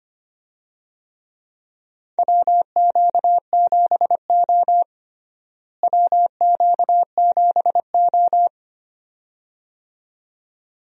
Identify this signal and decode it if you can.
Morse code, heard as a keyed tone, about 25 wpm, 700 Hz. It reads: WQ7O WQ7O
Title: USA Call Signs - 4 Characters - Repeated 2x - 25wpm